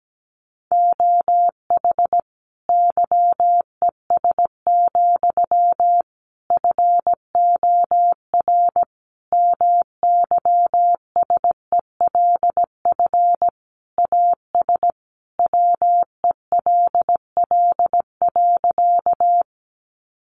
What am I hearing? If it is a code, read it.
Morse code, heard as a keyed tone, about 17 wpm, 700 Hz. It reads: OH YES, FOR MYSELF AS WELL.
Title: Sentences from Top 1300 Words 17wpm